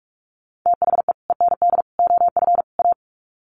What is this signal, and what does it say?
Morse code, heard as a keyed tone, about 44 wpm, 700 Hz. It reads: THE END OF A
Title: Sets of 4 Words 45wpm